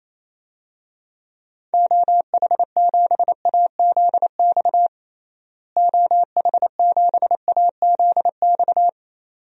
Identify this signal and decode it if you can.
Morse code, heard as a keyed tone, about 28 wpm, 700 Hz. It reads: OH7AZX OH7AZX
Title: DXCC-100 Call Signs - 6 Characters - Repeated 2x - 28wpm